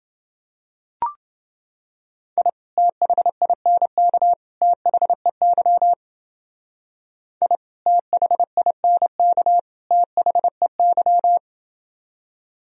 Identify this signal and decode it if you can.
Morse code, heard as a keyed tone, about 29 wpm, 700 Hz. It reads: I THINK THEY I THINK THEY